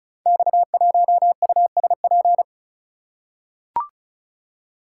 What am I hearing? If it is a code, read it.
Morse code, heard as a keyed tone, about 36 wpm, 690 Hz. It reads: X1USP
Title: Code Groups - 5 Letter:Numbers - 35wpm